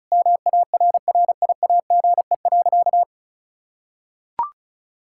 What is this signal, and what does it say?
Morse code, heard as a keyed tone, about 34 wpm, 700 Hz. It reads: MARRIAGE.
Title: Sentences from Top 1000 Words - Review Part 6 - 35wpm